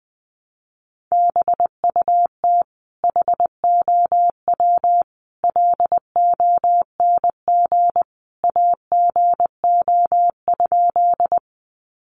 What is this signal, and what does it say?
Morse code, heard as a keyed tone, about 20 wpm, 700 Hz. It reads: BUT HOW LONG AGO?
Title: Sentences from Top 500 Words 20wpm